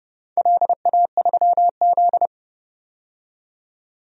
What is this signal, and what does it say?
Morse code, heard as a keyed tone, about 32 wpm, 700 Hz. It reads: LA3Z
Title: DXCC-100 Call Signs - 4 Characters - 30wpm